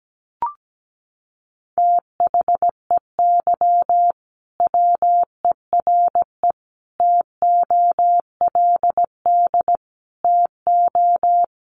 Morse code, heard as a keyed tone, about 17 wpm, 700 Hz. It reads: THEY WERE TOLD TO
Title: Sentences from Top 1300 Words 17wpm